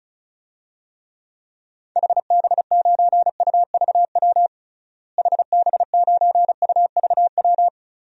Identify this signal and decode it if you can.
Morse code, heard as a keyed tone, about 36 wpm, 700 Hz. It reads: HB9UVW HB9UVW
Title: DXCC-100 Call Signs - 6 Characters - Repeated 2x - 35wpm